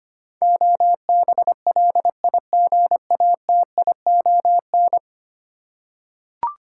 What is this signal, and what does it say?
Morse code, heard as a keyed tone, about 25 wpm, 700 Hz. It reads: OBLIGATION
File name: Ten Letter Words - 25wpm